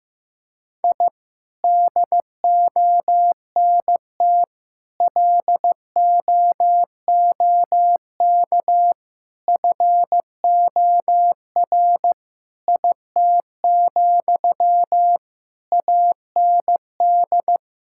Morse code, heard as a keyed tone, about 16 wpm, 700 Hz. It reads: I DONT LOOK FOR IT, AND
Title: Sentences from Top 1000 Words - Review Part 5 - 15wpm